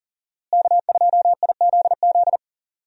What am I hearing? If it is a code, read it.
Morse code, heard as a keyed tone, about 40 wpm, 700 Hz. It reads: K2IZZ
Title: USA Call Signs - 5 Characters - Repeated 2x - 40wpm